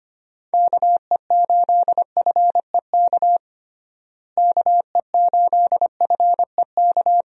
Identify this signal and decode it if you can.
Morse code, heard as a keyed tone, about 25 wpm, 700 Hz. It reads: KE8FEK KE8FEK
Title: USA Call Signs - 6 Characters - Repeated 3x - 25wpm